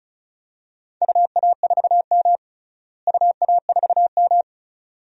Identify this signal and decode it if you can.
Morse code, heard as a keyed tone, about 36 wpm, 700 Hz. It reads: UA4M UA4M
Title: DXCC-100 Call Signs - 4 Characters - Repeated 2x - 35wpm